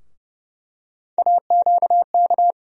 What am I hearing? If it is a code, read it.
Morse code, heard as a keyed tone, about 30 wpm, 700 Hz. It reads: AQK
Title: USA Call Sign Postfixes - 3 Letters 30wpm